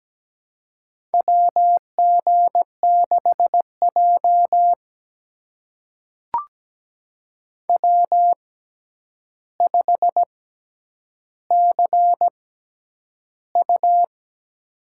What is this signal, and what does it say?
Morse code, heard as a keyed tone, about 17 wpm, spaced as for 9 wpm, 700 Hz. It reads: WG6J W5CU
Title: USA Call Signs - 4 Characters - 6x Character Spacing - 17wpm